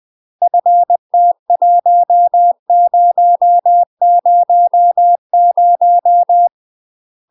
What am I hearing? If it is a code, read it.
Morse code, heard as a keyed tone, about 20 wpm, 700 Hz. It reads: FT1000
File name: UK MASTER QSOs TRAINING / 20WPM / UK MASTER QSOs 0022